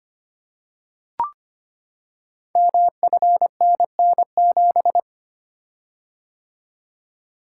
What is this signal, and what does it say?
Morse code, heard as a keyed tone, about 25 wpm, 700 Hz. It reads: MFNN7
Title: Code Groups - 5 Letter:Numbers - 25wpm